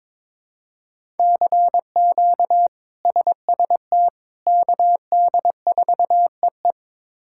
Morse code, heard as a keyed tone, about 22 wpm, 700 Hz. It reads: CQ SST KD4EE
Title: K1USN SST QSO Practice - 22wpm